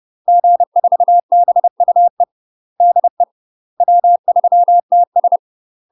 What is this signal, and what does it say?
Morse code, heard as a keyed tone, about 30 wpm, 700 Hz. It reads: G4BUE DE W3TS
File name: UK MASTER QSOs TRAINING / 30WPM / UK MASTER QSOs 0021